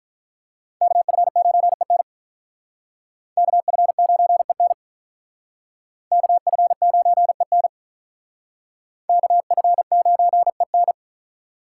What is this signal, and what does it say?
Morse code, heard as a keyed tone, about 41 wpm, 700 Hz. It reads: KF9EN KF9EN KF9EN KF9EN
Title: Speed Racing - USA Call Signs 35wpm